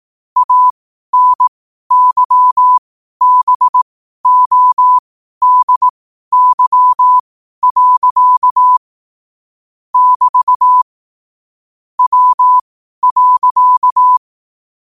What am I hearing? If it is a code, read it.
Morse code, heard as a keyed tone, about 18 wpm, spaced as for 12 wpm, 1000 Hz. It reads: ANYBODY. <BT> W.
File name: qotdfiles / 12wpm